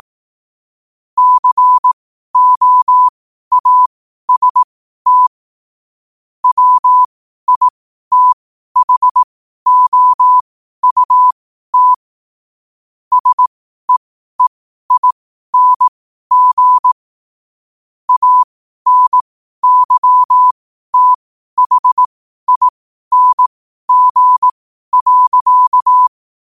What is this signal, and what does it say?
Morse code, heard as a keyed tone, about 18 wpm, spaced as for 12 wpm, 1000 Hz. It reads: COAST WITHOUT SEEING ANYTHING.
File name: qotdfiles / 12wpm